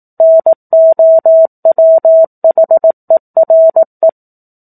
Morse code, heard as a keyed tone, about 18 wpm, 650 Hz. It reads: NOWHERE
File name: 04-Chapter-04 18 WPM